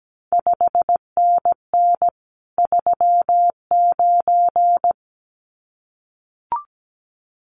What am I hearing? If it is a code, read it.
Morse code, heard as a keyed tone, about 17 wpm, 700 Hz. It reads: 5NN 39
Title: CQ WW DX Contest Exchange - 17wpm